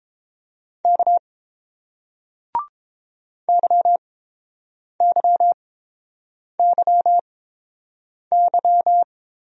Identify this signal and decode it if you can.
Morse code, heard as a keyed tone, about 30 wpm, spaced as for 7 wpm, 700 Hz. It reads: K YYYY